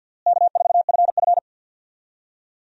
Morse code, heard as a keyed tone, about 46 wpm, 700 Hz. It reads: K4FF
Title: USA Call Signs - 4 Characters - Repeated 3x - 50wpm